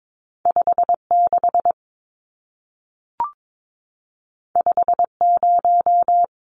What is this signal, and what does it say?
Morse code, heard as a keyed tone, about 22 wpm, 700 Hz. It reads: 56 50
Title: Numbers - 2-Digit 22wpm